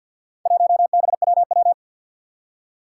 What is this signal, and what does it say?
Morse code, heard as a keyed tone, about 49 wpm, 700 Hz. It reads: 1BWW
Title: Four Challenging Letter-Number Combinations 50wpm